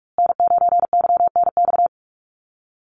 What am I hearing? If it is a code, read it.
Morse code, heard as a keyed tone, about 44 wpm, 690 Hz. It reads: N9YNX
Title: USA Call Signs Repeated 2x 45wpm